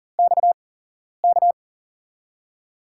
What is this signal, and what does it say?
Morse code, heard as a keyed tone, about 40 wpm, spaced as for 11 wpm, 700 Hz. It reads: XK
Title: USA Call Signs - 6 Characters - 8x Character Spacing - 40wpm